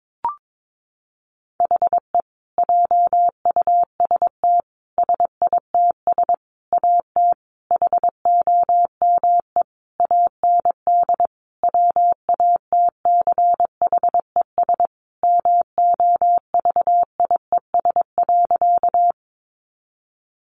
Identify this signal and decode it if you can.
Morse code, heard as a keyed tone, about 22 wpm, 700 Hz. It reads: HE JUST SITS AT HOME AND WATCHES MOVIES.
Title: Additional Easy Phrases 22wpm